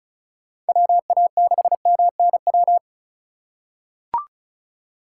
Morse code, heard as a keyed tone, about 36 wpm, 700 Hz. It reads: WA6MNW
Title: USA Call Signs - 6 Characters - 4x Character Spacing - 35wpm